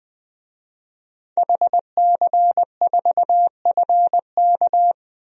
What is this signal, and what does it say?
Morse code, heard as a keyed tone, about 20 wpm, 700 Hz. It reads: HC4FK
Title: DXCC-100 Call Signs - 5 Characters - 20wpm